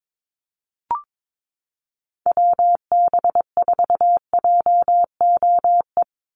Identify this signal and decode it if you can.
Morse code, heard as a keyed tone, about 22 wpm, 700 Hz. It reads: WB4JOE
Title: USA Call Signs 22wpm